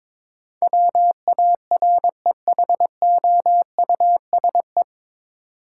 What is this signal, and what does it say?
Morse code, heard as a keyed tone, about 22 wpm, 700 Hz. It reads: WAREHOUSE
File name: 9 Letter Words - First 3 Letters Given - 22wpm